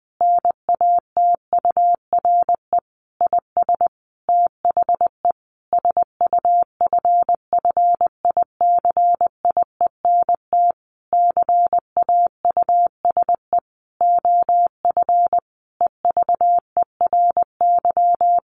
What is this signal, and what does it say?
Morse code, heard as a keyed tone, about 20 wpm, 700 Hz. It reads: NATURE IS THE SUFFICIENT CAUSE OF EVERY
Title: Sentences from Top 1400 Words 20wpm